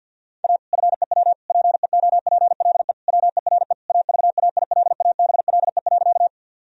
Morse code, heard as a keyed tone, about 48 wpm, 700 Hz. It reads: A FEW PEOPLE WERE AVAILABLE.
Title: Sentences from Top 1000 Words - Review Part 3 - 50wpm